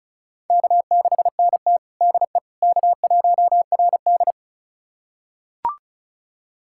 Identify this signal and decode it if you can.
Morse code, heard as a keyed tone, about 36 wpm, 700 Hz. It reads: K6NT DE K1RD
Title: USA Callsign DE Callsign - 4 Characters - 35wpm